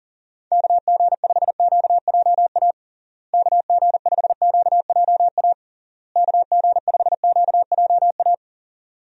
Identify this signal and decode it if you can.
Morse code, heard as a keyed tone, about 37 wpm, 700 Hz. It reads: KG5QJA KG5QJA KG5QJA
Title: USA Call Signs - 6 Characters - Repeated 3x - 40wpm